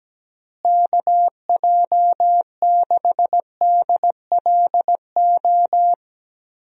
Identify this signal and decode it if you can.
Morse code, heard as a keyed tone, about 17 wpm, 700 Hz. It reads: KJ6DLO
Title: USA Call Signs - 6 Characters - Repeated 3x - 17wpm